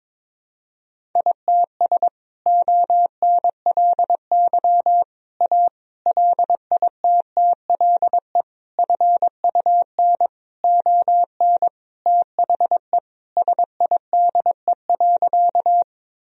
Morse code, heard as a keyed tone, about 22 wpm, 700 Hz. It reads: ITS ONLY A LITTLE FUN ON THE SIDE.
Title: Sentences from Top 1000 Words - Review Part 4 - 22wpm